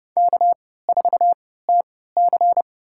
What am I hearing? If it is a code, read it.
Morse code, heard as a keyed tone, about 30 wpm, spaced as for 17 wpm, 700 Hz. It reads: K4TC
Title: USA Call Signs - 4 Characters - 3x Character Spacing - 30wpm